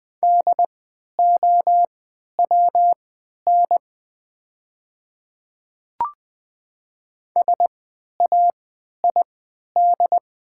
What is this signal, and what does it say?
Morse code, heard as a keyed tone, about 20 wpm, spaced as for 11 wpm, 700 Hz. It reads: DOWN SAID
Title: 3x Character Spacing - Top 200 Words - 20wpm